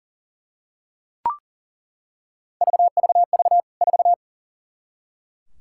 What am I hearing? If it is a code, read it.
Morse code, heard as a keyed tone, about 40 wpm, 700 Hz. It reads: VVV 4
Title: K1USN SST QSO Practice 40wpm